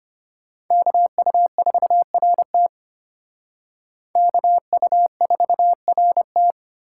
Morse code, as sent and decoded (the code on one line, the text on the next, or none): -.- ..- ....- .-. - / -.- ..- ....- .-. -
KU4RT KU4RT